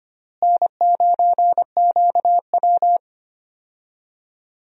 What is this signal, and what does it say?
Morse code, heard as a keyed tone, about 25 wpm, 700 Hz. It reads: N9QW
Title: USA Call Signs - 4 Characters - Repeated 2x - 25wpm